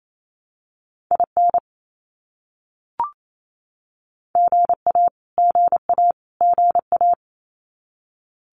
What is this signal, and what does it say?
Morse code, heard as a keyed tone, about 28 wpm, 700 Hz. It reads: IN GA GA GA
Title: US States Repeated 3x 28wpm